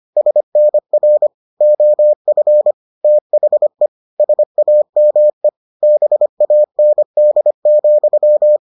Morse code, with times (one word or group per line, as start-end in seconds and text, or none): 0.2-1.3 SNR
1.6-2.7 OF
3.0-3.9 THE
4.2-5.5 SAME
5.8-8.6 BAND,